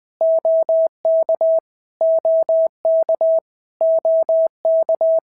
Morse code, heard as a keyed tone, about 20 wpm, 650 Hz. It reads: OK OK OK
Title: POTA QSO Practice - 20wpm